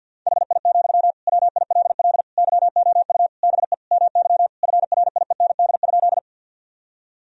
Morse code, heard as a keyed tone, about 50 wpm, 700 Hz. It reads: HI, WILL YOU BE MY FRIEND?
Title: Sentences from Top 1000 Words - Review Part 3 - 50wpm